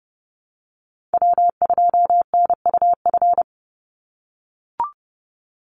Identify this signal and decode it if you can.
Morse code, heard as a keyed tone, about 30 wpm, 700 Hz. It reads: W2NUF
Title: USA Call Signs 30wpm